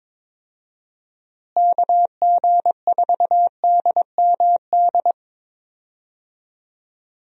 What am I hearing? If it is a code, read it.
Morse code, heard as a keyed tone, about 22 wpm, 700 Hz. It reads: KG4DMD
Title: USA Call Signs - 6 Characters - Repeated 3x - 22wpm